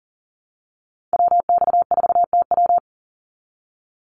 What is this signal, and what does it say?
Morse code, heard as a keyed tone, about 40 wpm, 700 Hz. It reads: WX4TW
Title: NA SOTA Activator Callsigns 40wpm